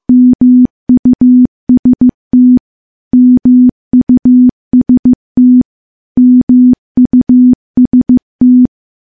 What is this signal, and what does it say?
Morse code, heard as a keyed tone, about 15 wpm, 250 Hz. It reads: MUST MUST MUST
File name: morse-training-15WPM